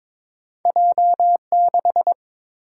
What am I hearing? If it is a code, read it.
Morse code, heard as a keyed tone, about 22 wpm, 700 Hz. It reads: J6
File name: practice / 22wpm / Two Letter-Number Combinations - 22wpm